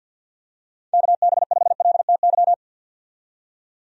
Morse code, heard as a keyed tone, about 46 wpm, 700 Hz. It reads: KB5LTY
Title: USA Call Signs - 6 Characters - 50wpm